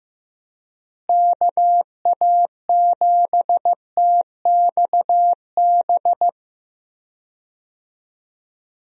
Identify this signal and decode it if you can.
Morse code, heard as a keyed tone, about 15 wpm, 700 Hz. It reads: KA7TXB
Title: USA Call Signs - 6 Characters - Repeated 2x - 15wpm